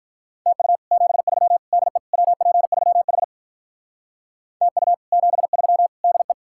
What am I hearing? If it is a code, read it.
Morse code, heard as a keyed tone, about 45 wpm, 700 Hz. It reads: TU 73 DE WP3H TU 73 DE